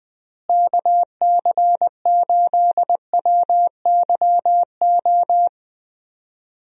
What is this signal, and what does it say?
Morse code, heard as a keyed tone, about 20 wpm, 700 Hz. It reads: KC8WYO
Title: USA Call Signs - 6 Characters - Repeated 2x - 20wpm